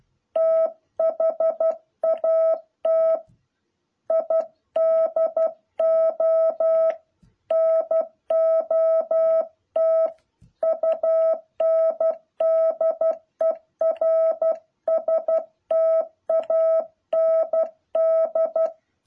Morse code, heard as a keyed tone, about 12 wpm, 650 Hz. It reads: THAT I DO NOT UNDERSTAND
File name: jh8cbh / 2021cw